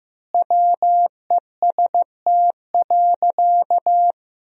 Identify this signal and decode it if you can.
Morse code, heard as a keyed tone, about 15 wpm, 700 Hz. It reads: WEST.
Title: 3x Word-Spacing - Sentences from Top 400 Words - 15wpm